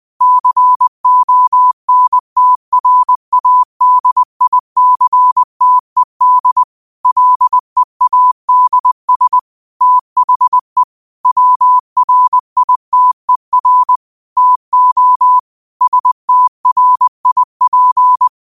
Morse code, heard as a keyed tone, about 20 wpm, 1000 Hz. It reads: CONTRADICTED LEADS THE WRITER TO STRIP
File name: qotdfiles / 20wpm